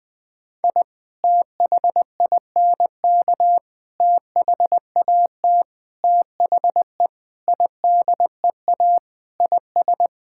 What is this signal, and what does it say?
Morse code, heard as a keyed tone, about 20 wpm, 700 Hz. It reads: I THINK THAT THE IDEA IS
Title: Sentences with 4-Word Phrases - Spoken Beginning - 20wpm